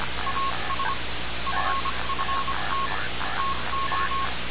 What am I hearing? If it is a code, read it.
Morse code, heard as a keyed tone, about 25 wpm, spaced as for 22 wpm, 1090 Hz, with noise. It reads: AS FFT TO
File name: radiocap / REC z